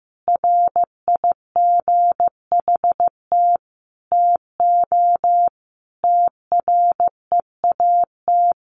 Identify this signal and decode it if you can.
Morse code, heard as a keyed tone, about 15 wpm, 700 Hz. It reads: RIGHT TO TREAT
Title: Sentences from Top 1300 Words 15wpm